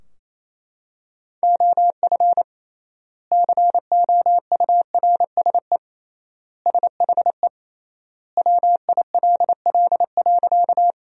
Morse code, heard as a keyed tone, about 28 wpm, 700 Hz. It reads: OF COURSE SHE WILL.
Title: 3x Word-Spacing - Sentences from Top 500 Words - 28wpm